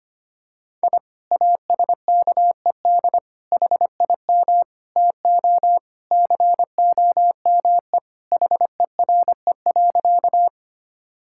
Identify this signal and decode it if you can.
Morse code, heard as a keyed tone, about 25 wpm, 700 Hz. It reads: I ASKED HIM TO COME HERE.
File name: practice / 25wpm / Sentences with 2-Word Phrases - 25wpm